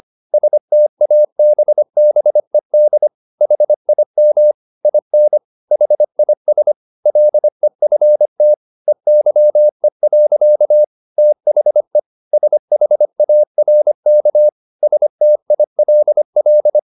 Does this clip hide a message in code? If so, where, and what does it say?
Yes: Morse code from 0.3 to 16.8 s, heard as a keyed tone, about 25 wpm, 600 Hz. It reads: STABBED HIM IN HIS LEFT EYE. THE SHARK STILL